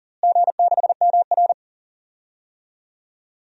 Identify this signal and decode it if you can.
Morse code, heard as a keyed tone, about 40 wpm, 700 Hz. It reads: G6MR
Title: DXCC-100 Call Signs - 4 Characters - Repeated 2x - 40wpm